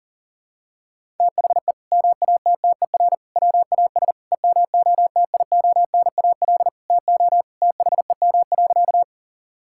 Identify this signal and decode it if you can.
Morse code, heard as a keyed tone, about 40 wpm, 700 Hz. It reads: THE MATTER WAS EMOTIONAL TO THEM.
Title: Sentences from Top 1000 Words - Review Part 5 - 40wpm